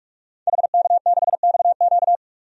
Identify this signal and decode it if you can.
Morse code, heard as a keyed tone, about 45 wpm, 700 Hz. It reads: HK6XQ
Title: DXCC-100 Call Signs - 5 Characters - 45wpm